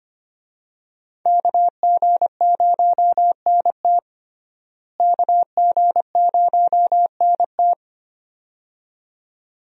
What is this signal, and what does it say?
Morse code, heard as a keyed tone, about 25 wpm, 700 Hz. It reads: KG0NT KG0NT
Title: USA Call Signs - 5 Characters - Repeated 2x - 25wpm